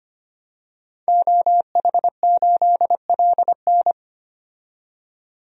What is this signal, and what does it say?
Morse code, heard as a keyed tone, about 25 wpm, 700 Hz. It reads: OH8LN
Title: DXCC-100 Call Signs - 5 Characters - 25wpm